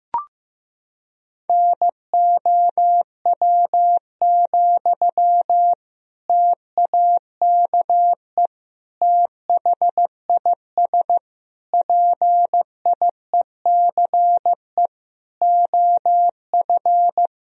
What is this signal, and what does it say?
Morse code, heard as a keyed tone, about 15 wpm, 700 Hz. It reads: NOW, TAKE THIS PIECE OF
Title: Sentences from Top 1000 Words - Review Part 2 - 15wpm